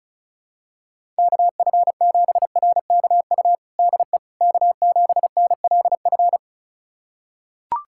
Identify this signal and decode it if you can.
Morse code, heard as a keyed tone, about 36 wpm, 700 Hz. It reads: KF7RKU DE K7NLF
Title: USA Callsign DE Callsign 35wpm